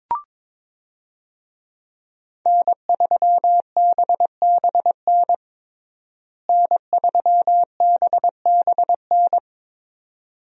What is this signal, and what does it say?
Morse code, heard as a keyed tone, about 22 wpm, 700 Hz. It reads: N3BBN N3BBN
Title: USA Call Signs - 5 Characters - Repeated 2x - 22wpm